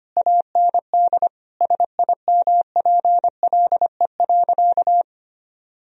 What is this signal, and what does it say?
Morse code, heard as a keyed tone, about 25 wpm, 700 Hz. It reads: AND SIMPLE.
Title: Sentences from Top 1000 Words - Review Part 3 - 25wpm